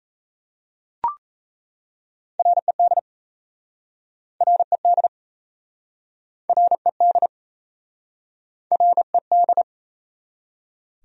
Morse code, heard as a keyed tone, about 37 wpm, 700 Hz. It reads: RED RED RED RED